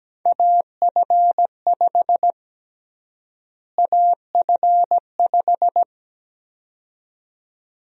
Morse code, heard as a keyed tone, about 17 wpm, 700 Hz. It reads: AF5 AF5
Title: USA Call Sign Prefixes - Repeated 2x - 17wpm